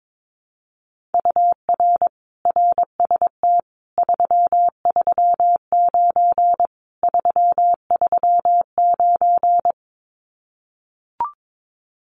Morse code, heard as a keyed tone, about 22 wpm, 700 Hz. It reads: UR RST 339 339
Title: Common QSO Phrases Encore 22wpm